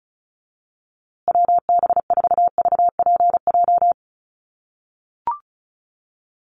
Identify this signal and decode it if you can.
Morse code, heard as a keyed tone, about 34 wpm, 700 Hz. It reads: WB4VPJ
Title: USA Call Signs Repeated 2x 35wpm